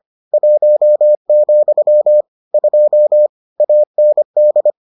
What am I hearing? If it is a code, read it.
Morse code, heard as a keyed tone, about 25 wpm, 600 Hz. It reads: 1, 2 AND